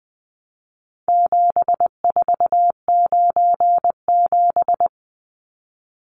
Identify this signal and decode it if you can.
Morse code, heard as a keyed tone, about 20 wpm, 700 Hz. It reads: 7497
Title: Numbers - 4-Digit 20wpm